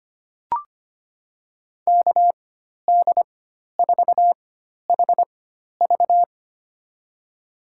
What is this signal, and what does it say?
Morse code, heard as a keyed tone, about 25 wpm, spaced as for 12 wpm, 700 Hz. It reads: KD4HV
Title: USA Call Signs - 5 Characters - 4x Character Spacing - 25wpm